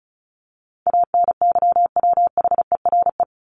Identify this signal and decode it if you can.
Morse code, heard as a keyed tone, about 34 wpm, 700 Hz. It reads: ANYWHERE
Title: Top 1251-1500 Words 35wpm